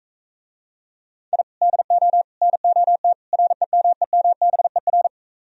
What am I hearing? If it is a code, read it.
Morse code, heard as a keyed tone, about 45 wpm, 700 Hz. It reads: I DO NOT REMEMBER